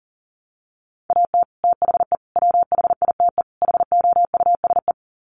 Morse code, heard as a keyed tone, about 40 wpm, 700 Hz. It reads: AT THE WHITE HOUSE
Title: Sets of 4 Words 40wpm